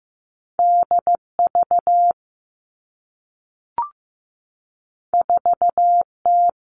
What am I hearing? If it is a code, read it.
Morse code, heard as a keyed tone, about 15 wpm, 700 Hz. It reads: D V 4 T
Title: Two Challenging Letter-Number Combinations 15wpm